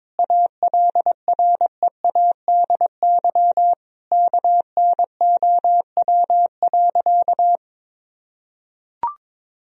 Morse code, heard as a keyed tone, about 22 wpm, 700 Hz. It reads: ALREADY KNOW.
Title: Sentences from Top 1000 Words - Review Part 1 - 22wpm